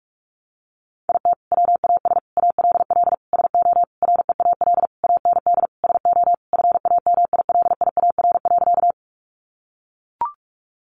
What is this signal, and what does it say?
Morse code, heard as a keyed tone, about 44 wpm, 700 Hz. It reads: IT WAS ALL SO REAL AND SO FAMILIAR.
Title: Sentences from Top 1000 Words 45wpm